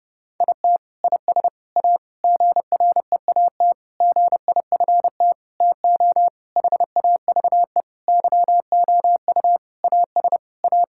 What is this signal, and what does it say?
Morse code, heard as a keyed tone, about 31 wpm, 700 Hz. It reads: IT IS A GREAT GIFT TO HAVE YOU AS A
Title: Sentences from Top 1000 Words - Review Part 6 - 30wpm